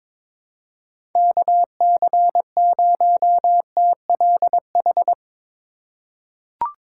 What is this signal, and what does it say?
Morse code, heard as a keyed tone, about 22 wpm, 700 Hz. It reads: KC0TLH
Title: USA Call Signs - 6 Characters - 4x Character Spacing - 22wpm